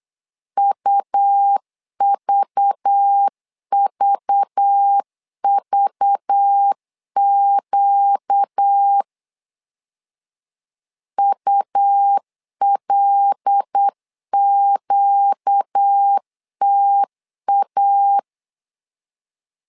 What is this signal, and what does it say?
Morse code, heard as a keyed tone, about 8 wpm, 800 Hz. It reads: UVVVQ ULQTA
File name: LEZIONI / 5 LBVC / 5 LBV 8wpm 2837